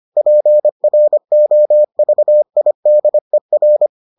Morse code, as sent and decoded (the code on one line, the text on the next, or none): .--. .-. --- ...- .. -.. .-.
PROVIDER